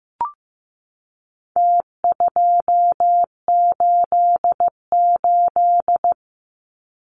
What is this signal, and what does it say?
Morse code, heard as a keyed tone, about 15 wpm, 700 Hz. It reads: T288
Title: Numbers - Contest Style - 4-Digit 15wpm